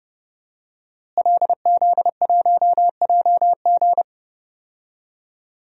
Morse code, heard as a keyed tone, about 30 wpm, 700 Hz. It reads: LZ1JG
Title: DXCC-100 Call Signs - 5 Characters - Repeated 2x - 30wpm